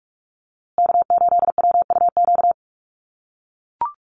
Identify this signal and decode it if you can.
Morse code, heard as a keyed tone, about 44 wpm, 700 Hz. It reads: K8WUQ
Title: USA Call Signs Repeated 3x 45wpm